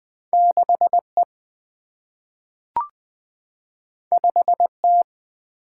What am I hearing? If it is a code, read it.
Morse code, heard as a keyed tone, about 20 wpm, 700 Hz. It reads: 6E 5T
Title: Single-Letter Single-Number 20wpm